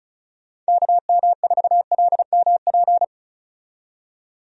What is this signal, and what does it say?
Morse code, heard as a keyed tone, about 35 wpm, 700 Hz. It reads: KM4LMP
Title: USA Call Signs - 6 Characters - 35wpm